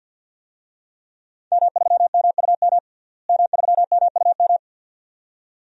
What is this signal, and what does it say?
Morse code, heard as a keyed tone, about 53 wpm, 700 Hz. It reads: M3MUM M3MUM
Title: DXCC-100 Call Signs - 5 Characters - Repeated 2x - 50wpm